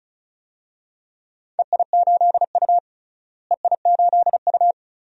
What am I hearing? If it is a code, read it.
Morse code, heard as a keyed tone, about 36 wpm, 700 Hz. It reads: EI8U EI8U
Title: DXCC-100 Call Signs - 4 Characters - Repeated 2x - 35wpm